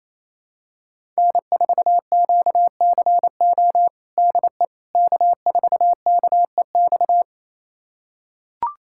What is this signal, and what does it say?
Morse code, heard as a keyed tone, about 28 wpm, 700 Hz. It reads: N4QCO DE K4KEX
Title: USA Callsign DE Callsign 28wpm